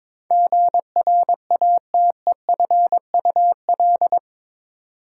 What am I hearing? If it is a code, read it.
Morse code, heard as a keyed tone, about 22 wpm, 700 Hz. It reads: GRATEFUL
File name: practice / 22wpm / Eight Letter Words - 22wpm